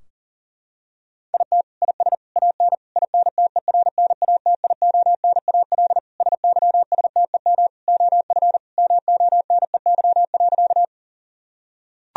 Morse code, heard as a keyed tone, about 40 wpm, 700 Hz. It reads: IT IS AN INTERNATIONAL SYSTEM OF MONEY.
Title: Sentences from Top 1000 Words - Review Part 4 - 40wpm